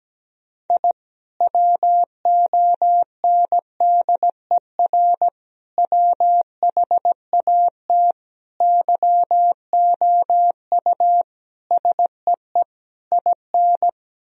Morse code, heard as a keyed tone, about 17 wpm, 700 Hz. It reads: I WONDER WHAT YOU SEE IN
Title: Sentences from Top 1000 Words - Review Part 3 - 17wpm